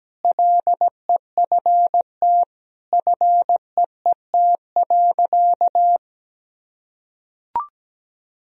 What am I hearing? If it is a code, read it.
Morse code, heard as a keyed tone, about 17 wpm, 700 Hz. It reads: LEFT FEET.
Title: Sentences from Top 1000 Words - Review Part 1 - 17wpm